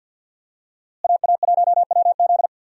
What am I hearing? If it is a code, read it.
Morse code, heard as a keyed tone, about 50 wpm, 700 Hz. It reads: AA1WZ
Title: USA Call Signs - 5 Characters - Repeated 2x - 50wpm